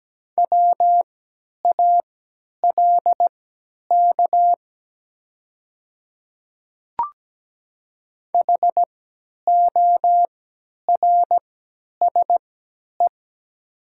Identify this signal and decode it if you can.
Morse code, heard as a keyed tone, about 17 wpm, spaced as for 10 wpm, 700 Hz. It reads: WALK HORSE
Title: 3x Character Spacing - Top 500 Words - 17wpm